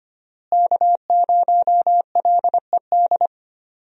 Morse code, heard as a keyed tone, about 25 wpm, 690 Hz. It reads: K0LED
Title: USA Call Signs - 5 Characters - Repeated 2x - 25wpm